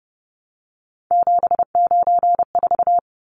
Morse code, heard as a keyed tone, about 30 wpm, 700 Hz. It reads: 794
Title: Numbers - 3-Digit 30wpm